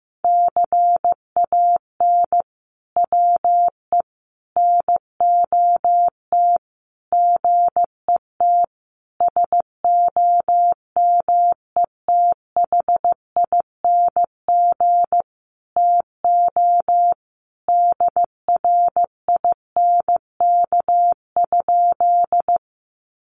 Morse code, heard as a keyed tone, about 15 wpm, 700 Hz. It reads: CAN WE NOT GET SOMETHING TO DRINK?
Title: Sentences from Top 1500 Words 15wpm